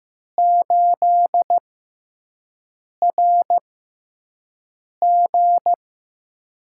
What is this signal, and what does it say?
Morse code, heard as a keyed tone, about 15 wpm, spaced as for 5 wpm, 700 Hz. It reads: 8RG
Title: USA Call Signs - 4 Characters - 6x Character Spacing - 15wpm